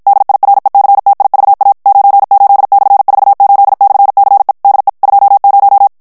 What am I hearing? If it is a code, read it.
Morse code, heard as a keyed tone, about 53 wpm, 800 Hz. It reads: DIRECTIVA 98/48/CE DE 20
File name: morse mp3 / ESPAGNOL TEXTE EN CLAIR 2 VITESSE 50 MOTS MINUTES